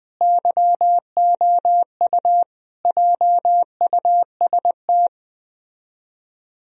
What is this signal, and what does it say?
Morse code, heard as a keyed tone, about 20 wpm, 700 Hz. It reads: YOU JUST
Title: Speed-Racing - Sets of 5 Words 17wpm